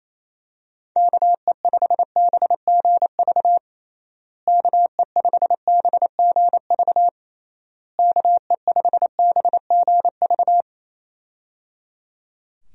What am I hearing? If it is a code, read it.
Morse code, heard as a keyed tone, about 28 wpm, 700 Hz. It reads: KE5BGV KE5BGV KE5BGV
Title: USA Call Signs - 6 Characters - Repeated 3x - 28wpm